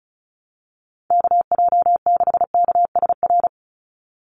Morse code, heard as a keyed tone, about 34 wpm, 700 Hz. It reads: KJ6KSR
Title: USA Call Signs 35wpm